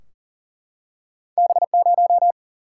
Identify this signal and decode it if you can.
Morse code, heard as a keyed tone, about 39 wpm, 700 Hz. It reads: B0
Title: Single-Letter Single-Number 40wpm